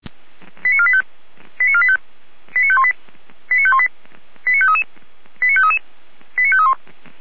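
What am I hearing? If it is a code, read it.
Five-tone selective call (ZVEI), heard as quick runs of five tones, tones of about 70 ms. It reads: 89485 89485 89518 89518 89533 89533 89531
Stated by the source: ZVEI five-tone call received by SDR.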